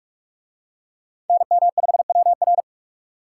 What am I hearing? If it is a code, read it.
Morse code, heard as a keyed tone, about 44 wpm, 700 Hz. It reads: NM5WR